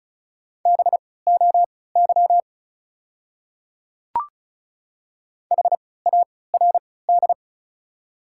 Morse code, heard as a keyed tone, about 36 wpm, spaced as for 20 wpm, 690 Hz. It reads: BOY HARD
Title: 3x Character Spacing - Top 300 Words - 35wpm